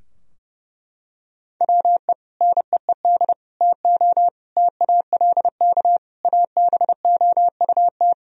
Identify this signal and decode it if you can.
Morse code, heard as a keyed tone, about 30 wpm, 700 Hz. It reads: WE NEED TO TALK ABOUT
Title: Sentences from Top 1000 Words - Review Part 4 - 30wpm